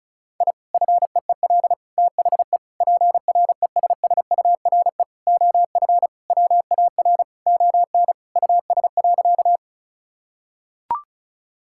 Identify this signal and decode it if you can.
Morse code, heard as a keyed tone, about 36 wpm, 700 Hz. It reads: I FEEL THE PRESSURE OF WAR ON US.
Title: Sentences from Top 1000 Words - Review Part 6 - 35wpm